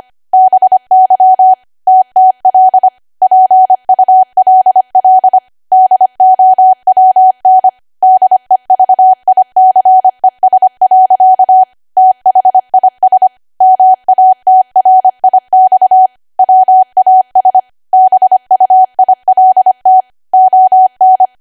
Morse code, heard as a keyed tone, about 25 wpm, 750 Hz. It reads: BY TTL PULL DOWN DEVICES. THIS MATRIX WAS BUILT ON